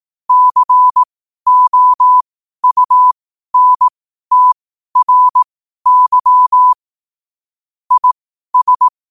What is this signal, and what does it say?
Morse code, heard as a keyed tone, about 18 wpm, spaced as for 12 wpm, 1000 Hz. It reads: COUNTRY IS
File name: qotdfiles / 12wpm